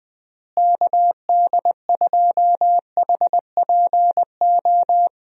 Morse code, heard as a keyed tone, about 20 wpm, 700 Hz. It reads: KD2HPO
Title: USA Call Signs - 6 Characters - Repeated 2x - 20wpm